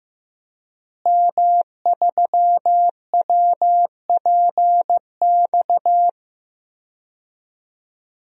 Morse code, heard as a keyed tone, about 15 wpm, 700 Hz. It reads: M3WPX
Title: DXCC-100 Call Signs - 5 Characters - Repeated 3x - 15wpm